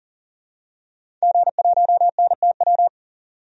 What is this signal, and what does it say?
Morse code, heard as a keyed tone, about 40 wpm, 700 Hz. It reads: G1NTW
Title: DXCC-100 Call Signs - 5 Characters - 40wpm